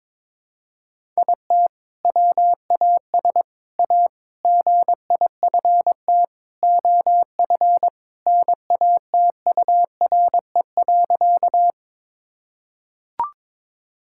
Morse code, heard as a keyed tone, about 22 wpm, 700 Hz. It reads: IT WAS A GIFT OF NATURE.
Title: Sentences from Top 1000 Words - Review Part 6 - 22wpm